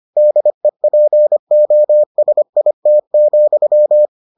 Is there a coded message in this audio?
Yes: Morse code from 0.2 to 4.1 s, heard as a keyed tone, about 25 wpm, 600 Hz. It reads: DEPOSIT,